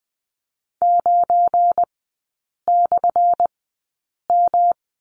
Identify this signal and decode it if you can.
Morse code, heard as a keyed tone, about 20 wpm, spaced as for 8 wpm, 700 Hz. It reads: 9/M
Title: 218 Lesson 35e - Number 0 - Three Characters 20wpm